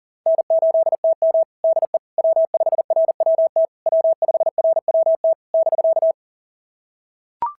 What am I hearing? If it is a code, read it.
Morse code, heard as a keyed tone, about 40 wpm, 650 Hz. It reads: N8TM DE W5RWT W5RWT <BK>
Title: SOTA QSO Practice 40wpm